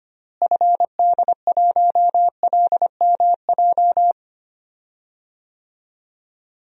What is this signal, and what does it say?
Morse code, heard as a keyed tone, about 25 wpm, 700 Hz. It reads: FD1LMJ
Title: DXCC-100 Call Signs - 6 Characters - Repeated 2x - 25wpm